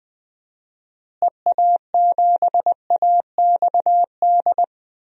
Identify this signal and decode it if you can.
Morse code, heard as a keyed tone, about 21 wpm, 700 Hz. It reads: EA7AXD
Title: DXCC-100 Call Signs - 6 Characters - Repeated 3x - 20wpm